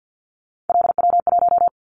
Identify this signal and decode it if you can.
Morse code, heard as a keyed tone, about 50 wpm, 700 Hz. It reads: LW1
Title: Callsign Prefixes for Top 50 DXCC - 50wpm